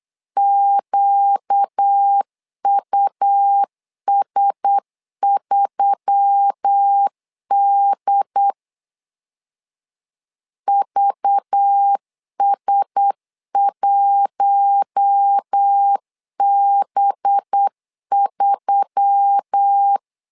Morse code, heard as a keyed tone, about 8 wpm, 800 Hz. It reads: QUS3D VS1B3
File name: LEZIONI / 5 LBVC / 5 LBV 8wpm 2837